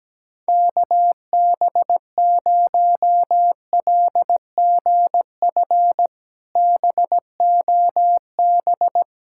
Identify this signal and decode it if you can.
Morse code, heard as a keyed tone, about 17 wpm, 700 Hz. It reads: KB0LGF BOB
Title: K1USN SST QSO Practice - 17wpm